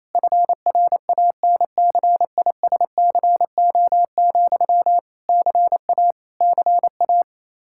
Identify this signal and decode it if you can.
Morse code, heard as a keyed tone, about 28 wpm, 700 Hz. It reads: FRANCISCO, CA CA
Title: Practice QSO - Exchange 1 - Beginning - 28wpm